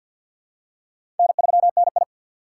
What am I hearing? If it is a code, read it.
Morse code, heard as a keyed tone, about 48 wpm, 700 Hz. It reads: N3NI
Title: USA Call Signs - 2x Character Spacing - 50wpm